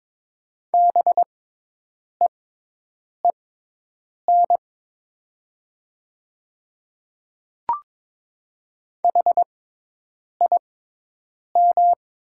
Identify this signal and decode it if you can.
Morse code, heard as a keyed tone, about 22 wpm, spaced as for 8 wpm, 700 Hz. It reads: BEEN HIM
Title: 6x Character Spacing - Top 100 Words - 22wpm